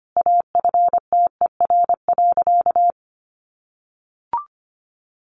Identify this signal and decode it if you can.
Morse code, heard as a keyed tone, about 25 wpm, 700 Hz. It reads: AFTER.
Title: Sentences from Top 1500 Words 25wpm